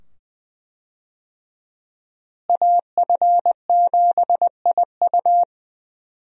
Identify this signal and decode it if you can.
Morse code, heard as a keyed tone, about 20 wpm, 700 Hz. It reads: AF7IU
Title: USA Call Signs - 5 Characters - Repeated 2x - 20wpm